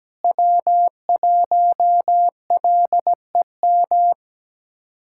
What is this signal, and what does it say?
Morse code, heard as a keyed tone, about 17 wpm, 700 Hz. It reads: W1LEM
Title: K1USN SST QSO Practice - 17wpm